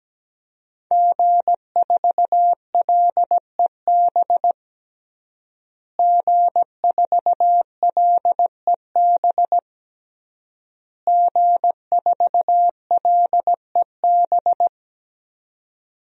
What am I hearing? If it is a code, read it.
Morse code, heard as a keyed tone, about 17 wpm, 700 Hz. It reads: G4LEB G4LEB G4LEB
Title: DXCC-100 Call Signs - 5 Characters - Repeated 3x - 17wpm